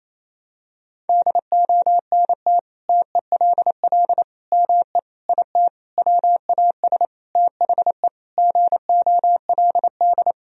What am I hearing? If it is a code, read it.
Morse code, heard as a keyed tone, about 28 wpm, 700 Hz. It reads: DONT TELL ME IT WAS THE GOLD
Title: Sentences from Top 1000 Words - Review Part 3 - 28wpm